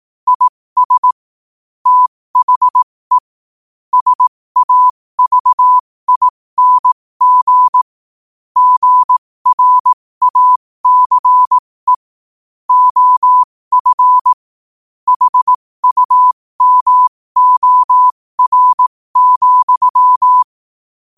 Morse code, heard as a keyed tone, about 18 wpm, spaced as for 15 wpm, 1000 Hz. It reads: IS THE SAVING GRACE OF HUMOR,
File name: qotdfiles / 15wpm